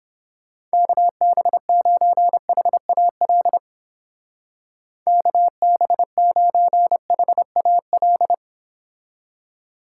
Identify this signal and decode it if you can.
Morse code, heard as a keyed tone, about 29 wpm, 700 Hz. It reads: KB9HAL KB9HAL